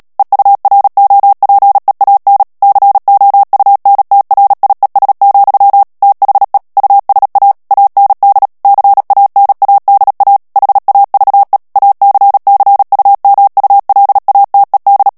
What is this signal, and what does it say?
Morse code, heard as a keyed tone, about 37 wpm, 800 Hz. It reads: EUROPEAN COUNTRIES, THE USA AND CANADA HAVE ACCUMULATED
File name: morse mp3 / ANGLAIS texte clair 2 vitesse 35 mots minute